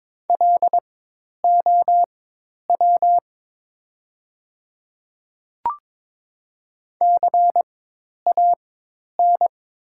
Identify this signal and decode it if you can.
Morse code, heard as a keyed tone, about 22 wpm, spaced as for 10 wpm, 690 Hz. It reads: LOW CAN
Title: 4x Character Spacing - Top 300 Words - 22wpm